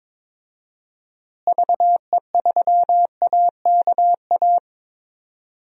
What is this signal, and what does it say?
Morse code, heard as a keyed tone, about 22 wpm, 700 Hz. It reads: VE3AKA
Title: DXCC-100 Call Signs - 6 Characters - Repeated 3x - 22wpm